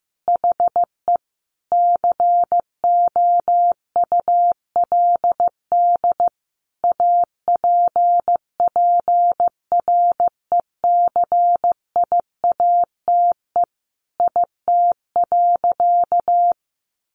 Morse code, heard as a keyed tone, about 15 wpm, 700 Hz. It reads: HE COULD APPRECIATE IT.
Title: Sentences from Top 1300 Words 15wpm